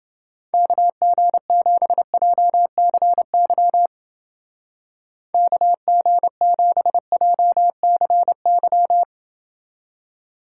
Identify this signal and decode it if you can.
Morse code, heard as a keyed tone, about 29 wpm, 700 Hz. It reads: KG7JCY KG7JCY